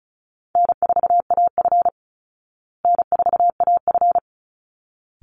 Morse code, heard as a keyed tone, about 34 wpm, 700 Hz. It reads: N4AF N4AF
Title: Easy Call Signs Repeated 2x - 35wpm